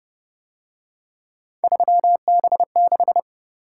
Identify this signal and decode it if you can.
Morse code, heard as a keyed tone, about 30 wpm, 700 Hz. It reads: 3B6
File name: Speed-Racing - Three Letter-Number Combinations - 25wpm